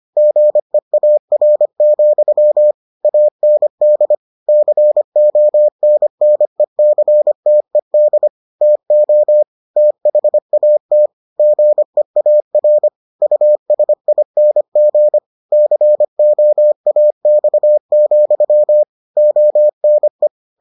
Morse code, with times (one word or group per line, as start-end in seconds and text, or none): 0.2-2.7 GEAR,
3.0-4.2 AND
4.5-8.3 CONNECTED
8.6-9.4 TO
9.8-11.1 THAT
11.4-12.9 GEAR
13.2-15.2 USING
15.5-18.8 COAX,
19.2-20.3 ONE